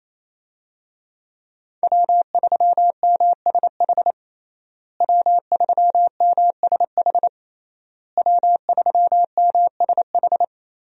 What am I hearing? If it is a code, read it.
Morse code, heard as a keyed tone, about 28 wpm, 700 Hz. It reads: W3MSH W3MSH W3MSH
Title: USA Call Signs - 5 Characters - Repeated 3x - 28wpm